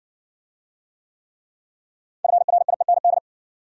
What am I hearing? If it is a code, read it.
Morse code, heard as a keyed tone, about 58 wpm, 700 Hz. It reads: FRIEND